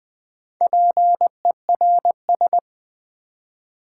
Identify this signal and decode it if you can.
Morse code, heard as a keyed tone, about 20 wpm, 700 Hz. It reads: PERS
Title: Build A Word - 2 Letters At A Time - 8-Letter Words - 20wpm